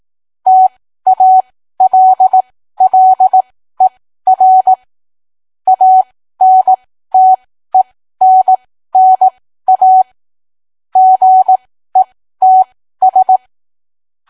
Morse code, heard as a keyed tone, about 18 wpm, spaced as for 13 wpm, 750 Hz. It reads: TALLER ANTENNA GETS